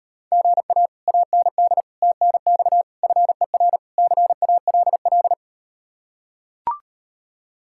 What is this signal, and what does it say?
Morse code, heard as a keyed tone, about 39 wpm, 700 Hz. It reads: GA AND TNX FER CALL